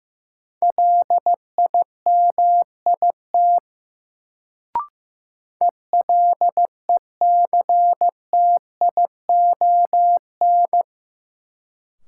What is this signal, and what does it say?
Morse code, heard as a keyed tone, about 16 wpm, 700 Hz. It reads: LIMIT ELECTION
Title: Sentences from Top 1500 Words - New Word Drill - 15wpm